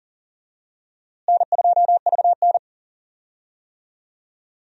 Morse code, heard as a keyed tone, about 37 wpm, 700 Hz. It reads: N2VN
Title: USA Call Signs - 4 Characters - Repeated 2x - 40wpm